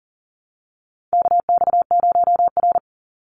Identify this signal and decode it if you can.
Morse code, heard as a keyed tone, about 40 wpm, 700 Hz. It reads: KX0R
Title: NA SOTA Chaser Callsigns 40wpm